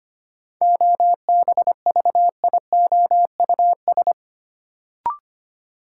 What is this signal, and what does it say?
Morse code, heard as a keyed tone, about 25 wpm, 700 Hz. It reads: OBVIOUS
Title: Sentences from Top 1600 Words - New Word Drill - 25wpm